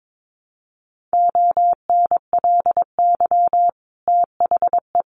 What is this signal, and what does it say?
Morse code, heard as a keyed tone, about 22 wpm, 700 Hz. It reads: ONLY THE
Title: Sentences from Top 800 Words 22wpm